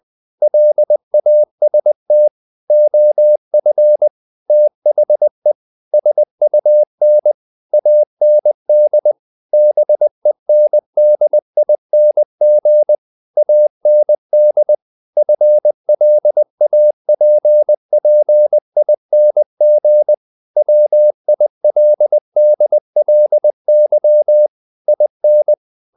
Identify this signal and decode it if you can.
Morse code, heard as a keyed tone, about 20 wpm, 600 Hz. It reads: LAST OF THE SUN AND BENDING AND FLAPPING WILDLY IN
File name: the old man and the sea part 2 20wpm